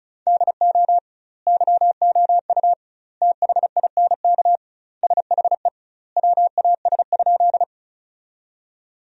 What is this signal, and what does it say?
Morse code, heard as a keyed tone, about 36 wpm, 700 Hz. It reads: DO YOU THINK SHE WAS?
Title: 2x Word-Spacing - Sentences from Top 200 Words - 35wpm